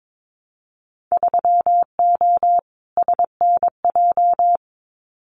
Morse code, heard as a keyed tone, about 22 wpm, 700 Hz. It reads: 3O SNJ
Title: Winter Field Day Exchanges Encore 22wpm